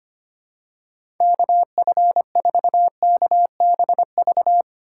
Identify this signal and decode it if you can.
Morse code, heard as a keyed tone, about 25 wpm, 700 Hz. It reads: KF4KBV
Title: USA Call Signs - 6 Characters - 2x Character Spacing - 25wpm